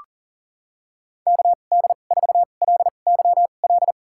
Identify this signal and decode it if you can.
Morse code, heard as a keyed tone, about 40 wpm, spaced as for 29 wpm, 700 Hz. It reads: KD4LYL
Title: USA Call Signs - 6 Characters - 2x Character Spacing - 40wpm